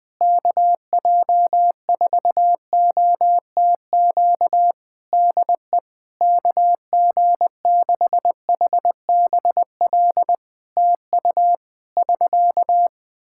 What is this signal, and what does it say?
Morse code, heard as a keyed tone, about 21 wpm, 700 Hz. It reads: KJ4OTQ DE KG6HBL TU <SK>
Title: Practice QSO - Exchange 4 - Ending - 20wpm